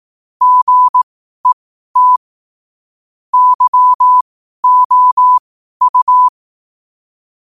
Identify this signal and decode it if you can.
Morse code, heard as a keyed tone, about 18 wpm, spaced as for 12 wpm, 1000 Hz. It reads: GET YOU
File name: qotdfiles / 12wpm